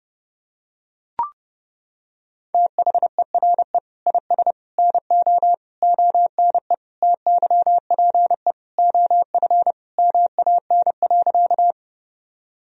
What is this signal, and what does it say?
Morse code, heard as a keyed tone, about 32 wpm, 700 Hz. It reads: THERE IS NO ONE TYPE OF MAN.
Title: Sentences from Top 1000 Words - Review Part 3 - 30wpm